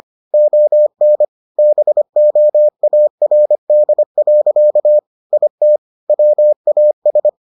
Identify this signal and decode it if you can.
Morse code, heard as a keyed tone, about 25 wpm, 600 Hz. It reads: ON BOARD. IT WAS